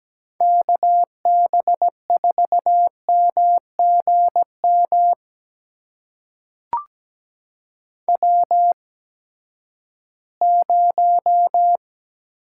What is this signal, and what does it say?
Morse code, heard as a keyed tone, about 17 wpm, spaced as for 12 wpm, 700 Hz. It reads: KB4MGM W0
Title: USA Call Signs - 8x Character Spacing - 17wpm